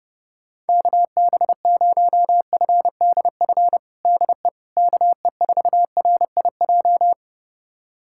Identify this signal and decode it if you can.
Morse code, heard as a keyed tone, about 32 wpm, 700 Hz. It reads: KB0FDF DE KE4RIJ
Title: USA Callsign DE Callsign 30wpm